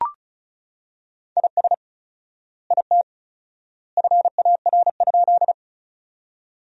Morse code, heard as a keyed tone, about 34 wpm, 700 Hz. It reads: IS IT FAR?
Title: 4x Word-Spacing - Sentences from Top 300 Words - 35wpm